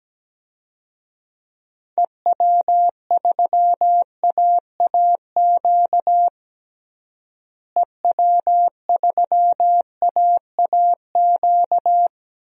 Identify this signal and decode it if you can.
Morse code, heard as a keyed tone, about 17 wpm, 700 Hz. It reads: EW3AAQ EW3AAQ
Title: DXCC-100 Call Signs - 6 Characters - Repeated 2x - 17wpm